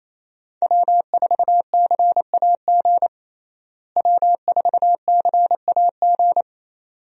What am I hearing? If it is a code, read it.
Morse code, heard as a keyed tone, about 28 wpm, 700 Hz. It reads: W4CAG W4CAG
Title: USA Call Signs - 5 Characters - Repeated 2x - 28wpm